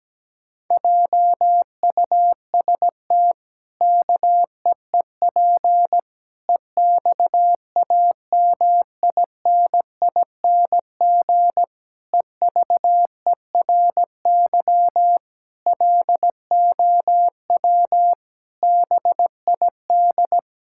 Morse code, heard as a keyed tone, about 17 wpm, 700 Hz. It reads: JUST KEEP EXAMINING EVERY LOW BID
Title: Pangram Challenge v2 - 17wpm